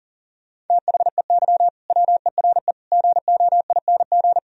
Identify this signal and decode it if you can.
Morse code, heard as a keyed tone, about 52 wpm, 700 Hz. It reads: THEY WERE GOING